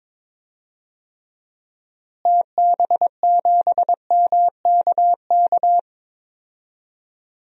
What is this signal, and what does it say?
Morse code, heard as a keyed tone, about 22 wpm, 700 Hz. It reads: TB7MKK
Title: DXCC-100 Call Signs - 6 Characters - Repeated 2x - 22wpm